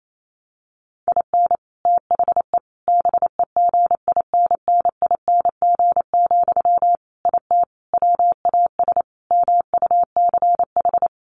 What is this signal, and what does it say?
Morse code, heard as a keyed tone, about 28 wpm, 700 Hz. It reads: IN THE BEGINNING, IT WAS MUCH
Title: Sentences from Top 1400 Words 28wpm